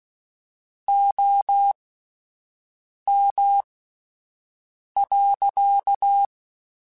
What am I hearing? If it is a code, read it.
Morse code, heard as a keyed tone, about 16 wpm, spaced as for 5 wpm, 800 Hz. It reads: OM.
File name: QSOs Spaced